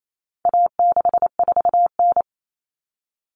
Morse code, heard as a keyed tone, about 28 wpm, 700 Hz. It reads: A64N
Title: Common Cut Numbers - 4-Digit 28wpm